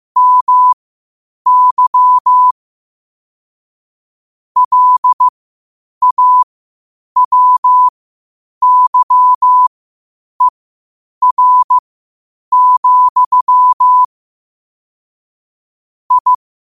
Morse code, heard as a keyed tone, about 15 wpm, spaced as for 8 wpm, 1000 Hz. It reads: MY LAWYER, I
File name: qotdfiles / 8wpm